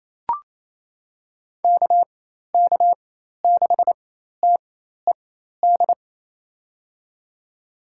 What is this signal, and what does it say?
Morse code, heard as a keyed tone, about 27 wpm, spaced as for 13 wpm, 700 Hz. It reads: KK6TED
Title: USA Call Signs - 4x Character Spacing - 28wpm